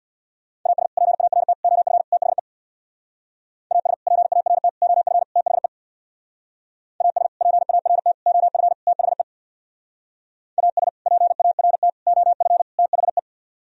Morse code, heard as a keyed tone, about 64 wpm, 700 Hz. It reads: AS PART OF THE AS PART OF THE AS PART OF THE AS PART OF THE